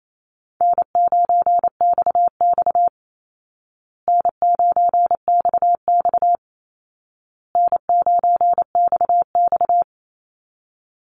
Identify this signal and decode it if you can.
Morse code, heard as a keyed tone, about 28 wpm, 700 Hz. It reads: N9XX N9XX N9XX
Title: Easy Call Signs Repeated 3x - 28wpm